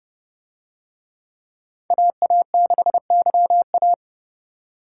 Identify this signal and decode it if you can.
Morse code, heard as a keyed tone, about 30 wpm, 700 Hz. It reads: AA6YA